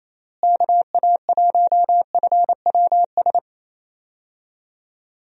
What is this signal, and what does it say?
Morse code, heard as a keyed tone, about 28 wpm, 700 Hz. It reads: KA1FWS
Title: USA Call Signs - 6 Characters - Repeated 2x - 28wpm